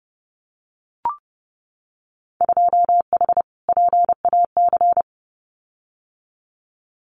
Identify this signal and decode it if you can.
Morse code, heard as a keyed tone, about 30 wpm, 700 Hz. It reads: 2H PAC
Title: Winter Field Day Exchanges Encore 30wpm